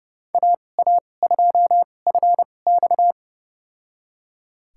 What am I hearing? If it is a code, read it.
Morse code, heard as a keyed tone, about 30 wpm, spaced as for 22 wpm, 700 Hz. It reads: AA2FX
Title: USA Call Signs - 5 Characters - 2x Character Spacing - 30wpm